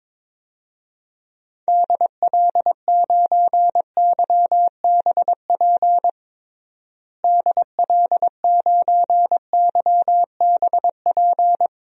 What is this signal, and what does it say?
Morse code, heard as a keyed tone, about 22 wpm, 700 Hz. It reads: DL9YBP DL9YBP
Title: DXCC-100 Call Signs - 6 Characters - Repeated 2x - 22wpm